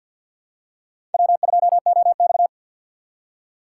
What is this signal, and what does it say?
Morse code, heard as a keyed tone, about 53 wpm, 700 Hz. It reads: W2OX
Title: USA Call Signs - 4 Characters - 2x Character Spacing - 50wpm